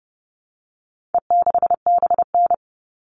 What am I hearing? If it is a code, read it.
Morse code, heard as a keyed tone, about 30 wpm, 700 Hz. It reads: E6BN
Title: All Cut Numbers - 4-Digit 30wpm